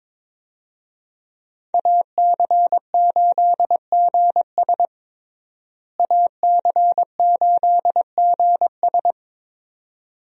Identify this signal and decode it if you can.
Morse code, heard as a keyed tone, about 22 wpm, 700 Hz. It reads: AC8GS AC8GS
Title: USA Call Signs - 5 Characters - Repeated 2x - 22wpm